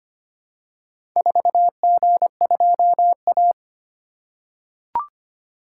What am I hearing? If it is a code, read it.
Morse code, heard as a keyed tone, about 25 wpm, 700 Hz. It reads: 4G2A
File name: practice / 25wpm / Four Letter-Number Combinations - 25wpm